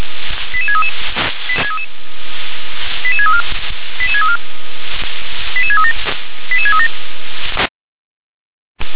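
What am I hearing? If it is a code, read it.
Five-tone selective call (ZVEI), heard as quick runs of five tones, tones of about 70 ms. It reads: 88633 88634 88634 88637 88637
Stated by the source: Five-tone ZVEI selective call received on SDR.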